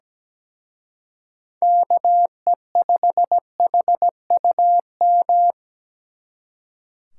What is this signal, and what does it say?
Morse code, heard as a keyed tone, about 20 wpm, 700 Hz. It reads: KE5HUM
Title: Speed Racing - USA 2x3 Call Signs 17wpm